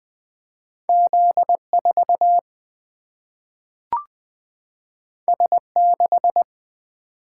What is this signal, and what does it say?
Morse code, heard as a keyed tone, about 20 wpm, 700 Hz. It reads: Z4 S6
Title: Single-Letter Single-Number 20wpm